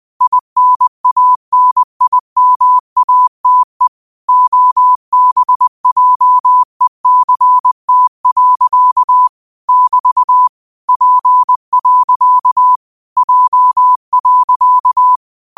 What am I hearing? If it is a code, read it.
Morse code, heard as a keyed tone, about 20 wpm, 1000 Hz. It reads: INANIMATE OBJECT. <BT> P. J.